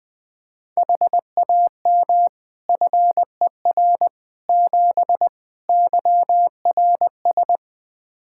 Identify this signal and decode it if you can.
Morse code, heard as a keyed tone, about 20 wpm, 700 Hz. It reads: HAM FER 7 YRS
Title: Practice QSO - Exchange 3 - More Info - 20wpm